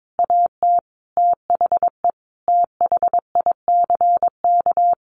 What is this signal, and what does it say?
Morse code, heard as a keyed tone, about 22 wpm, 700 Hz. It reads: AT THE THICK
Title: A Sound Of Thunder 22wpm